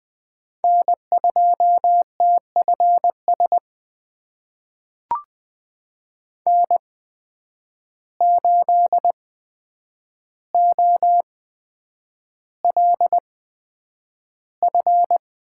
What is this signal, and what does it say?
Morse code, heard as a keyed tone, about 20 wpm, spaced as for 9 wpm, 700 Hz. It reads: N2TFS N8OLF
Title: USA Call Signs - 8x Character Spacing - 20wpm